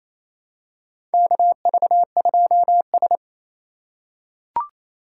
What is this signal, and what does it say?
Morse code, heard as a keyed tone, about 27 wpm, 700 Hz. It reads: KV2S
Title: USA Call Signs - 4 Characters - 3x Character Spacing - 28wpm